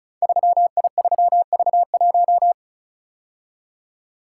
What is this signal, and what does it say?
Morse code, heard as a keyed tone, about 36 wpm, 700 Hz. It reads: 3I3V1
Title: Code Groups - 5 Letter:Numbers - 35wpm